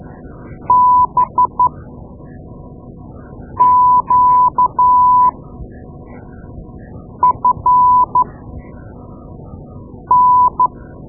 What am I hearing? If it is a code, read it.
Morse code, heard as a keyed tone, about 12 wpm, 1000 Hz. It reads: B Q F N
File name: Morse Lessons / Morse Lessons / Tape-2A